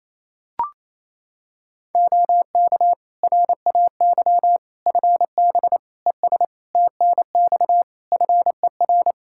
Morse code, heard as a keyed tone, about 28 wpm, 700 Hz. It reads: OK RAY FB ES TNX FER
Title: Practice QSO - Exchange 2 - Info - 28wpm